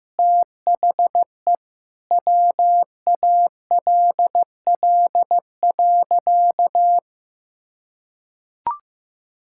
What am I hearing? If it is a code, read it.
Morse code, heard as a keyed tone, about 15 wpm, 700 Hz. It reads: THE WALL.
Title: Sentences from Top 1000 Words - Review Part 6 - 15wpm